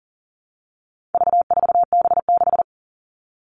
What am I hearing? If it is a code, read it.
Morse code, heard as a keyed tone, about 40 wpm, 700 Hz. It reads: V4B6
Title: All Cut Numbers - 4-Digit 40wpm